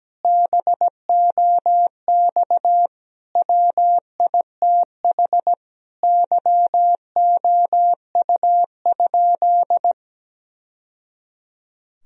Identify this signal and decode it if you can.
Morse code, heard as a keyed tone, about 17 wpm, 700 Hz. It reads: BOX WITH YOU?
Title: Sentences from Top 1000 Words - Review Part 2 - 17wpm